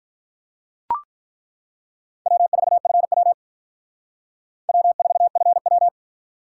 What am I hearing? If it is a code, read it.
Morse code, heard as a keyed tone, about 42 wpm, 700 Hz. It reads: W4FW W4FW
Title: Speed Racing - USA 1x2 Call Signs 35wpm